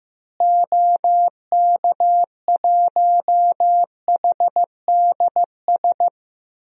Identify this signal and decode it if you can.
Morse code, heard as a keyed tone, about 15 wpm, 700 Hz. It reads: OK1HDS
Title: DXCC-100 Call Signs - 6 Characters - Repeated 3x - 15wpm